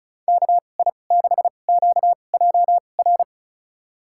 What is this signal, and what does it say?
Morse code, heard as a keyed tone, about 36 wpm, spaced as for 25 wpm, 700 Hz. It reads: KI6QJR
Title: USA Call Signs - 6 Characters - 2x Character Spacing - 35wpm